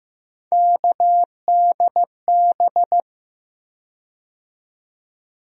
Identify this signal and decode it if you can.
Morse code, heard as a keyed tone, about 15 wpm, 700 Hz. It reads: KDB
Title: USA Call Sign Postfixes - 3 Letters 15wpm